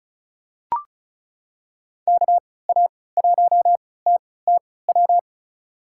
Morse code, heard as a keyed tone, about 36 wpm, spaced as for 20 wpm, 700 Hz. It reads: KA1TTW
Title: USA Call Signs - 6 Characters - 3x Character Spacing - 35wpm